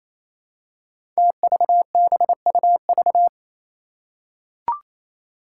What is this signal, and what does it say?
Morse code, heard as a keyed tone, about 28 wpm, 700 Hz. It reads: TVBUV
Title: Five Challenging Letter-Number Combinations 28wpm